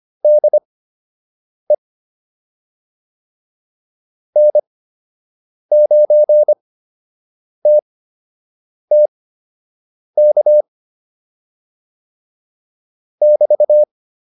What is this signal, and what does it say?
Morse code, heard as a keyed tone, about 25 wpm, spaced as for 7 wpm, 600 Hz. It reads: DE N9TTK <BT>